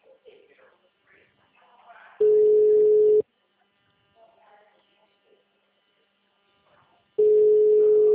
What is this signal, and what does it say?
Ringback tone (Europe, 425 Hz), with speech.